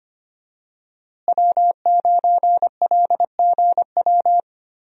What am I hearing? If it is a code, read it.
Morse code, heard as a keyed tone, about 25 wpm, 700 Hz. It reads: W9LGW
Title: USA Call Signs - 5 Characters - Repeated 2x - 25wpm